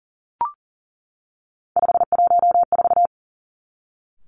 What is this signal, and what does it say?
Morse code, heard as a keyed tone, about 38 wpm, 700 Hz. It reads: 514
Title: Three Challenging Letter-Number Combinations 40wpm